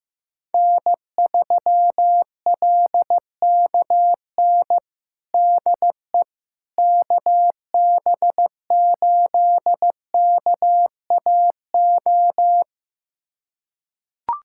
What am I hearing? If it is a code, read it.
Morse code, heard as a keyed tone, about 16 wpm, 700 Hz. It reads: N3LKN DE KB8KAO
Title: USA Callsign DE Callsign 15wpm